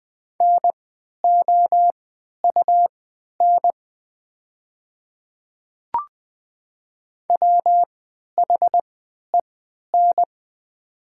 Morse code, heard as a keyed tone, about 20 wpm, spaced as for 11 wpm, 700 Hz. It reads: NOUN WHEN
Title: 3x Character Spacing - Top 500 Words - 20wpm